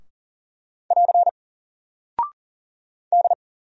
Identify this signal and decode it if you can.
Morse code, heard as a keyed tone, about 40 wpm, 700 Hz. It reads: <AR> D
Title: Single Character (Letter:Number:Punctuation:Prosign) Alt 40wpm